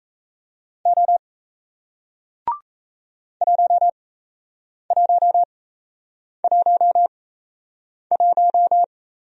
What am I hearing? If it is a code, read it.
Morse code, heard as a keyed tone, about 38 wpm, spaced as for 8 wpm, 700 Hz. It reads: O 1111